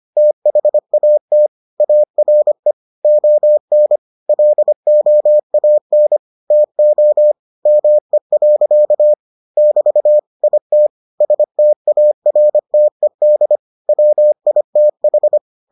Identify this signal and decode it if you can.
Morse code, heard as a keyed tone, about 25 wpm, 600 Hz. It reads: THAT ARE ON LOAN TO ME. <BT> IT STARTED WITH